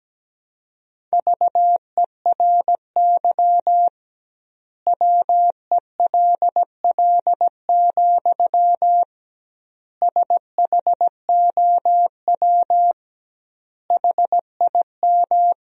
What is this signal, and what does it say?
Morse code, heard as a keyed tone, about 17 wpm, 700 Hz. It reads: VERY WELL, SHOW HIM
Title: 2x Word-Spacing - Sentences from Top 200 Words - 17wpm